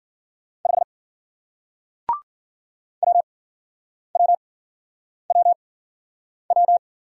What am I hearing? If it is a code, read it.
Morse code, heard as a keyed tone, about 50 wpm, spaced as for 9 wpm, 700 Hz. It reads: 5 WWWW
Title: Speed-Racing Single Letter-Number 40wpm